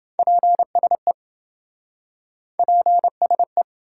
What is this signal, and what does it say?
Morse code, heard as a keyed tone, about 28 wpm, 700 Hz. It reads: PSE PSE